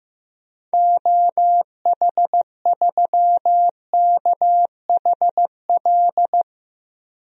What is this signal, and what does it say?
Morse code, heard as a keyed tone, about 15 wpm, 700 Hz. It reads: OH3KHL
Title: DXCC-100 Call Signs - 6 Characters - Repeated 3x - 15wpm